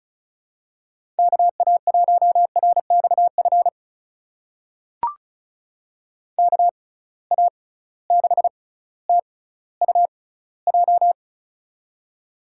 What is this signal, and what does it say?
Morse code, heard as a keyed tone, about 36 wpm, spaced as for 18 wpm, 700 Hz. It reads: KA1RXF KA6TUJ
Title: USA Call Signs - 6 Characters - 6x Character Spacing - 35wpm